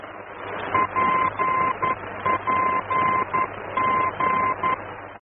Morse code, heard as a keyed tone, about 11 wpm, 1040 Hz, with noise. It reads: PPG
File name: france / mx / PPG-F-VOR